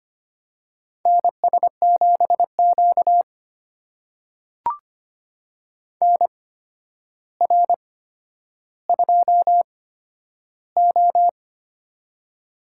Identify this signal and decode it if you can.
Morse code, heard as a keyed tone, about 25 wpm, spaced as for 11 wpm, 700 Hz. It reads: NS7Q NR2O
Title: USA Call Signs - 4 Characters - 8x Character Spacing - 25wpm